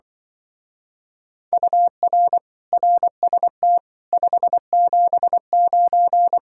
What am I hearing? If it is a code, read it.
Morse code, heard as a keyed tone, about 24 wpm, 700 Hz. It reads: UR RST 579
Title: Speed-Racing Common QSO Phrases - 20wpm